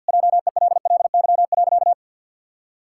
Morse code, heard as a keyed tone, about 50 wpm, 700 Hz. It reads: JELLY.
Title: Pangram Challenge v2 - 50wpm